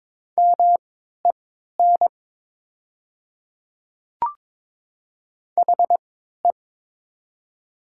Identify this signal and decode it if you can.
Morse code, heard as a keyed tone, about 22 wpm, spaced as for 13 wpm, 700 Hz. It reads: MEN HE
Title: 3x Character Spacing - Top 200 Words - 22wpm